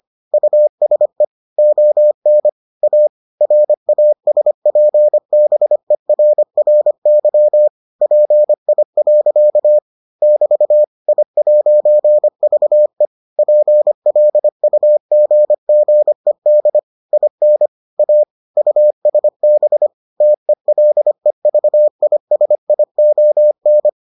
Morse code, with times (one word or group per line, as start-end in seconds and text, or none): 0.3-1.2 USE
1.6-2.5 ON
2.8-3.1 A
3.4-7.7 RASPBERRY
8.0-9.8 PI.
10.2-10.8 <BT>
11.1-13.1 I'VE
13.4-16.8 PLUGGED
17.1-17.7 IN
18.0-18.2 A
18.6-19.9 USB
20.2-23.9 TELEVISION